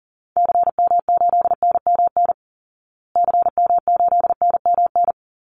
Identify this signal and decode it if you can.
Morse code, heard as a keyed tone, about 40 wpm, 700 Hz. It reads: CM8NMN CM8NMN
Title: Call Signs Repeated 2x - 40wpm